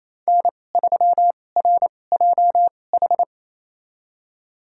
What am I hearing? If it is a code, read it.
Morse code, heard as a keyed tone, about 28 wpm, spaced as for 20 wpm, 700 Hz. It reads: N3RJH
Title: USA Call Signs - 5 Characters - 2x Character Spacing - 28wpm